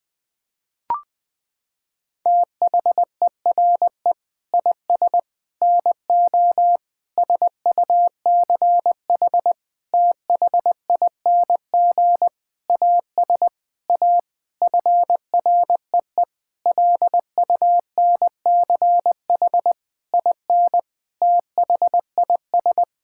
Morse code, heard as a keyed tone, about 20 wpm, 700 Hz. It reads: THERE IS NO SUCH THING AS A FREE LUNCH IN THIS
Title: Sentences with 9-Word Phrases - 20wpm